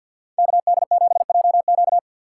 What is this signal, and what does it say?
Morse code, heard as a keyed tone, about 53 wpm, 700 Hz. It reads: KD7JQ
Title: USA Call Signs - 5 Characters - 6x Character Spacing - 50wpm